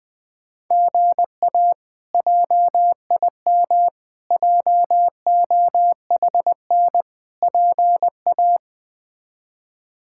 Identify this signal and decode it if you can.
Morse code, heard as a keyed tone, about 20 wpm, 700 Hz. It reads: GA JIM JOHN PA
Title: K1USN SST QSO Practice - 20wpm